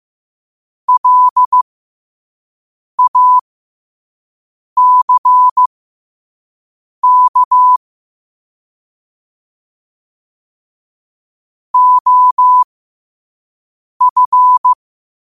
Morse code, heard as a keyed tone, about 15 wpm, spaced as for 5 wpm, 1000 Hz. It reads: LACK OF
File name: qotdfiles / 5wpm